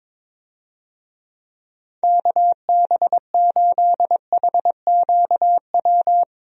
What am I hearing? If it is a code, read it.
Morse code, heard as a keyed tone, about 22 wpm, 700 Hz. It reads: KB8HQW
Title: USA Call Signs - 6 Characters - 22wpm